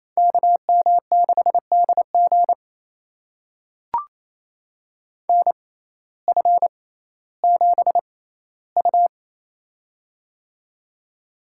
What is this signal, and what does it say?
Morse code, heard as a keyed tone, about 28 wpm, spaced as for 15 wpm, 700 Hz. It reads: KM6DG NF7U
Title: USA Call Signs - 6x Character Spacing - 28wpm